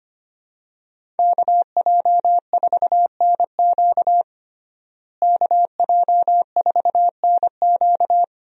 Morse code, heard as a keyed tone, about 25 wpm, 700 Hz. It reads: KJ4NQ KJ4NQ
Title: USA Call Signs - 5 Characters - Repeated 3x - 25wpm